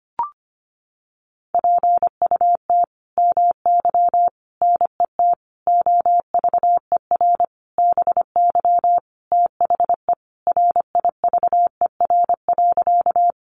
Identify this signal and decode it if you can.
Morse code, heard as a keyed tone, about 25 wpm, 700 Hz. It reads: PUT MY NET OVER BY THE RIVER.
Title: Sentences from Top 1400 Words 25wpm